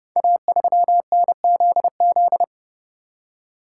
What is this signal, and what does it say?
Morse code, heard as a keyed tone, about 28 wpm, 700 Hz. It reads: A3NZZ
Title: Code Groups - 5 Letter:Numbers - 30wpm